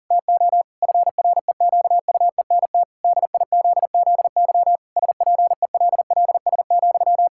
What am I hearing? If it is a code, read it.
Morse code, heard as a keyed tone, about 40 wpm, 700 Hz. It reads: TO FREQUENT DIZZY SPELLS,
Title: Pangram Challenge v2 - 40wpm